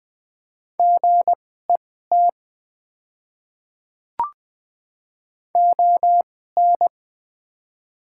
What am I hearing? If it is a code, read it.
Morse code, heard as a keyed tone, about 20 wpm, spaced as for 14 wpm, 700 Hz. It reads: GET ON
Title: 2x Character Spacing - Top 200 Words - 20wpm